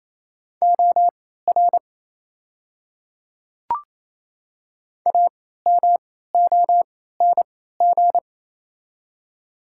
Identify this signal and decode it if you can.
Morse code, heard as a keyed tone, about 27 wpm, spaced as for 16 wpm, 700 Hz. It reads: OR AMONG
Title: 3x Character Spacing - Top 500 Words - 28wpm